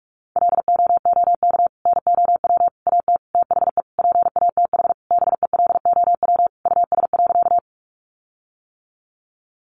Morse code, heard as a keyed tone, about 44 wpm, 700 Hz. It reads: LOOK NOW AT THE PATH BELOW US.
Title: Sentences from Top 1400 Words 45wpm